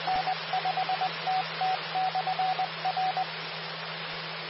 Morse code, heard as a keyed tone, about 20 wpm, 750 Hz. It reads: N5TT/R
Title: Austin, TX Amateur Repeaters